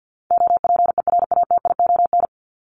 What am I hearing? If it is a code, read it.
Morse code, heard as a keyed tone, about 48 wpm, 700 Hz. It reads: OPERATION
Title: Top 1501-1750 Words 50wpm